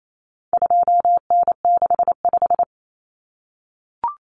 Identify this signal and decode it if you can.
Morse code, heard as a keyed tone, about 28 wpm, 700 Hz. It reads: 2N65
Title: Common Cut Numbers - 4-Digit 28wpm